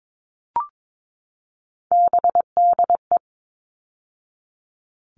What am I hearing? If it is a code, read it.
Morse code, heard as a keyed tone, about 22 wpm, 700 Hz. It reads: BDE
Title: All Cut Numbers - 3-Digit 22wpm